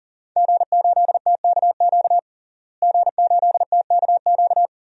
Morse code, heard as a keyed tone, about 40 wpm, 700 Hz. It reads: G8TKQ G8TKQ
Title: DXCC-100 Call Signs - 5 Characters - Repeated 2x - 40wpm